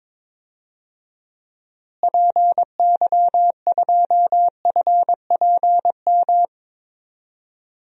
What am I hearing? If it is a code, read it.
Morse code, heard as a keyed tone, about 22 wpm, 700 Hz. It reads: PY2FPM
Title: DXCC-100 Call Signs - 6 Characters - Repeated 2x - 22wpm